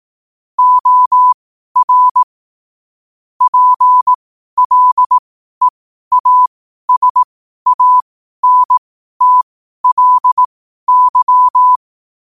Morse code, heard as a keyed tone, about 18 wpm, spaced as for 12 wpm, 1000 Hz. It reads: OR PLEASANTLY